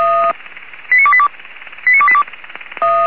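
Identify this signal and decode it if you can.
Five-tone selective call (ZVEI), heard as quick runs of five tones, tones of about 70 ms. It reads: 89282 89282
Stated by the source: ZVEI five-tone selective call received with an SDR.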